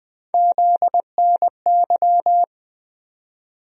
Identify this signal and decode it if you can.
Morse code, heard as a keyed tone, about 20 wpm, 700 Hz. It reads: ZNY
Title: USA Call Sign Postfixes - 3 Letters 20wpm